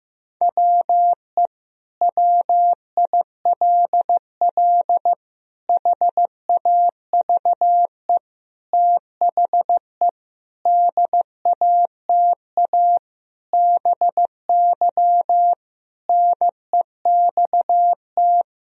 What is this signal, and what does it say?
Morse code, heard as a keyed tone, about 15 wpm, 700 Hz. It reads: WE WILL HAVE THE DATA BY NEXT
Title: Sentences from Top 1000 Words - Review Part 3 - 15wpm